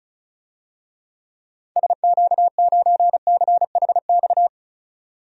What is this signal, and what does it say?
Morse code, heard as a keyed tone, about 36 wpm, 700 Hz. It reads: SQ9CHX
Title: DXCC-100 Call Signs - 6 Characters - 35wpm